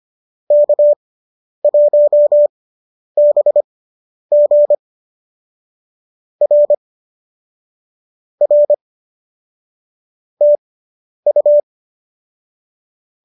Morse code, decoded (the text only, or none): K1BG R R TU